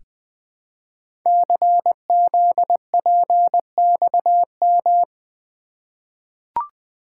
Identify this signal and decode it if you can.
Morse code, heard as a keyed tone, about 20 wpm, 700 Hz. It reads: CZPXM
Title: Code Groups - 5 Letters - 20wpm